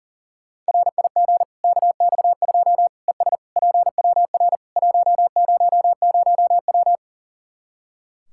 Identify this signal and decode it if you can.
Morse code, heard as a keyed tone, about 40 wpm, 700 Hz. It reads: RIG KX2 ES PWR 100W
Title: K1USN SST QSO Practice 40wpm